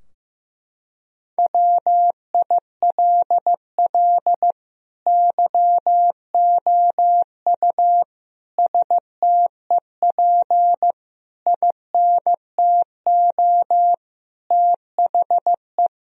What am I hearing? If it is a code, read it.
Morse code, heard as a keyed tone, about 16 wpm, 700 Hz. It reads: WILL YOU STEP INTO THE
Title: Sentences from Top 1000 Words - Review Part 2 - 15wpm